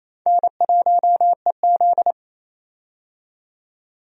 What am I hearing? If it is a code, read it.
Morse code, heard as a keyed tone, about 28 wpm, 700 Hz. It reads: N1EZ
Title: USA Call Signs - 4 Characters - 28wpm